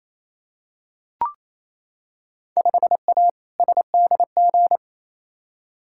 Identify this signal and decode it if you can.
Morse code, heard as a keyed tone, about 28 wpm, 700 Hz. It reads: 5A SDG
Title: Field Day Exchanges Encore - 28wpm